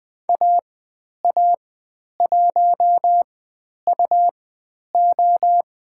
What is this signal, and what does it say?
Morse code, heard as a keyed tone, about 20 wpm, spaced as for 10 wpm, 700 Hz. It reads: AA1UO
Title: 102 Lesson 17g - Letter u - Call Signs 20wpm